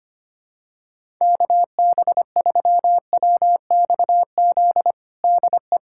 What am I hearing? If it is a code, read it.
Morse code, heard as a keyed tone, about 25 wpm, 700 Hz. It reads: KB3WXZ DE
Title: USA Callsign DE Callsign - 6 Characters - 25wpm